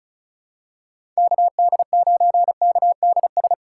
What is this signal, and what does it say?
Morse code, heard as a keyed tone, about 36 wpm, 700 Hz. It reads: KD9KDS
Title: USA Call Signs - 6 Characters - 4x Character Spacing - 35wpm